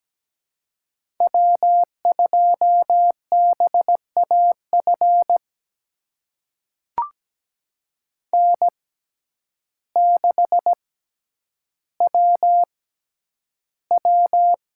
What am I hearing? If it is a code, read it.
Morse code, heard as a keyed tone, about 17 wpm, spaced as for 9 wpm, 700 Hz. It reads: W2BAF N6WW
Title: USA Call Signs - 5 Characters - 6x Character Spacing - 17wpm